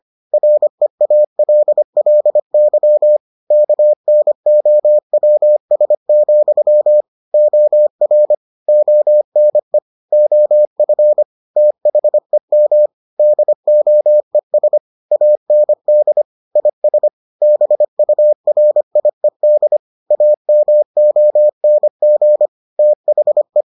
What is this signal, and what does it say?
Morse code, heard as a keyed tone, about 25 wpm, 600 Hz. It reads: REALLY KNOWS, OR ONE OF THEM DOES AND IS BURIED AMONG THE